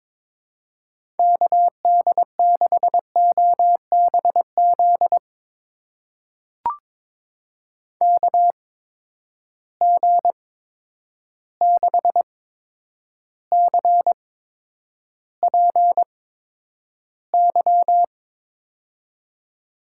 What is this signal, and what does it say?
Morse code, heard as a keyed tone, about 22 wpm, spaced as for 9 wpm, 700 Hz. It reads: KD6OBZ KG6CPY
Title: USA Call Signs - 6 Characters - 8x Character Spacing - 22wpm